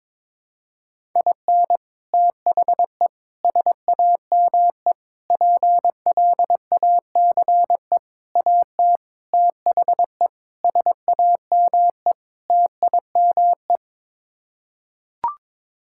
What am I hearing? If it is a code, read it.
Morse code, heard as a keyed tone, about 23 wpm, 700 Hz. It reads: IN THE SAME PLACE AT THE SAME TIME
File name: practice / 22wpm / Sets of 8 Words - 22wpm